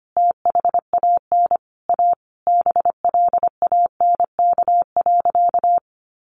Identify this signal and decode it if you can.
Morse code, heard as a keyed tone, about 25 wpm, 700 Hz. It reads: THAN A BLANK.
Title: Sentences from Top 1500 Words 25wpm